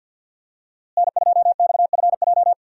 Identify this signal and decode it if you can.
Morse code, heard as a keyed tone, about 51 wpm, 700 Hz. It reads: N2XFJ
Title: USA Call Signs - 5 Characters - Repeated 3x - 50wpm